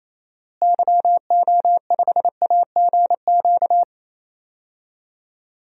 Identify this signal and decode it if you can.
Morse code, heard as a keyed tone, about 28 wpm, 700 Hz. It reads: YO5AGQ
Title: DXCC-100 Call Signs - 6 Characters - Repeated 2x - 28wpm